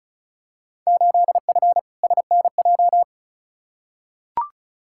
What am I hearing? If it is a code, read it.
Morse code, heard as a keyed tone, about 36 wpm, 690 Hz. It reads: 8F SNJ
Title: Field Day Exchanges Encore - 35wpm